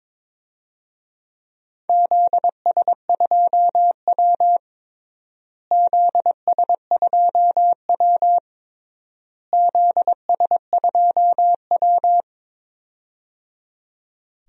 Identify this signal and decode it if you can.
Morse code, heard as a keyed tone, about 21 wpm, 700 Hz. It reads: ZS2W ZS2W ZS2W
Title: DXCC-100 Call Signs - 4 Characters - Repeated 3x - 22wpm